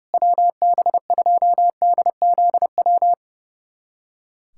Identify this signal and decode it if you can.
Morse code, heard as a keyed tone, about 30 wpm, 700 Hz. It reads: WB2DZW
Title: USA Callsign DE Callsign - 6 Characters - 30wpm